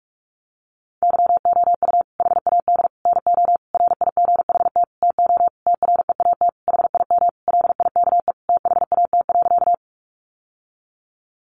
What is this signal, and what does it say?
Morse code, heard as a keyed tone, about 44 wpm, 700 Hz. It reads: YOU HAD NO RIGHT TO TREAT HIM LIKE THAT.
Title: Sentences from Top 1300 Words 45wpm